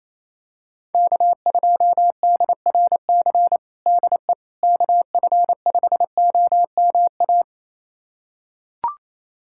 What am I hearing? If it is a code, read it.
Morse code, heard as a keyed tone, about 28 wpm, 700 Hz. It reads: K2DRC DE KF5OMA
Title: USA Callsign DE Callsign 28wpm